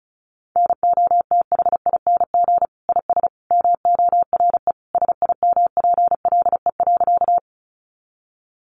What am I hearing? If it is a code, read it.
Morse code, heard as a keyed tone, about 34 wpm, 700 Hz. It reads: NOTHING IS MORE SIMPLE.
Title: Sentences from Top 500 Words 35wpm